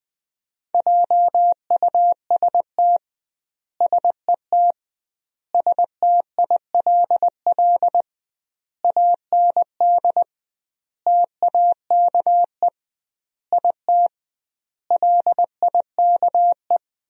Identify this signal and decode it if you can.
Morse code, heard as a keyed tone, about 21 wpm, 700 Hz. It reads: JUST SET STILL AND TAKE IT LIKE
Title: 2x Word-Spacing - Sentences from Top 300 Words - 20wpm